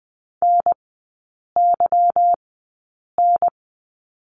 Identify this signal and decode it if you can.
Morse code, heard as a keyed tone, about 20 wpm, spaced as for 8 wpm, 700 Hz. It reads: NYN
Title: 152 Lesson 25e - Letter p - Three Characters 20wpm